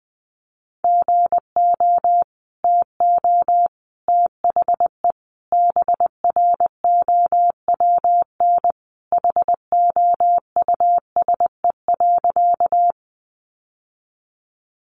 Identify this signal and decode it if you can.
Morse code, heard as a keyed tone, about 19 wpm, 700 Hz. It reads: GO TO THE BROWN HOUSE.
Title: Sentences from Top 1200 Words 20wpm